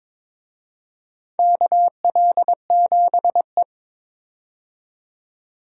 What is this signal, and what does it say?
Morse code, heard as a keyed tone, about 22 wpm, 700 Hz. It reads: KL7E
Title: DXCC-100 Call Signs - 4 Characters - 22wpm